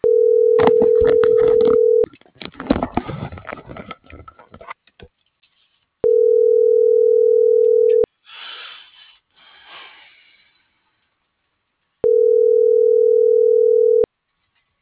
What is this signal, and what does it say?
Ringback tone (North America), 440 + 480 Hz.